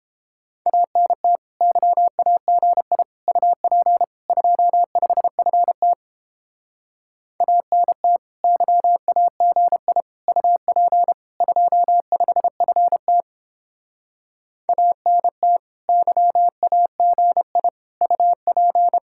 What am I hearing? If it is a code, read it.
Morse code, heard as a keyed tone, about 29 wpm, 700 Hz. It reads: ANT YAGI UP 25FT ANT YAGI UP 25FT ANT YAGI UP